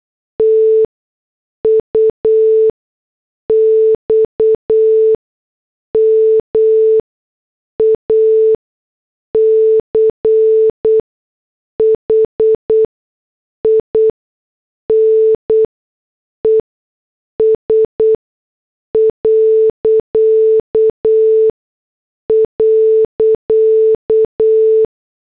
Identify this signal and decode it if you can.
Morse code, heard as a keyed tone, about 8 wpm, spaced as for 6 wpm, 440 Hz. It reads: TUX MACHINES ..